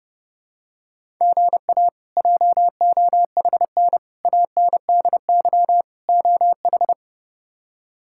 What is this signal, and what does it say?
Morse code, heard as a keyed tone, about 30 wpm, 700 Hz. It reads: GA JOHN ANDY OH
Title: K1USN SST QSO Practice - 30wpm